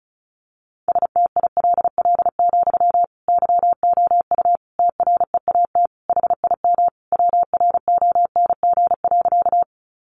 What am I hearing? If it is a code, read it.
Morse code, heard as a keyed tone, about 34 wpm, 700 Hz. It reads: STILL, YOU TREAT HIM WRONG.
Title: Sentences from Top 1300 Words 35wpm